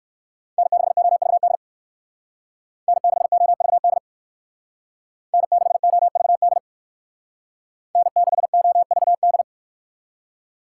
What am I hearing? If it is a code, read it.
Morse code, heard as a keyed tone, about 54 wpm, 700 Hz. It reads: N6OVD N6OVD N6OVD N6OVD
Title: Speed Racing - USA Call Signs 45wpm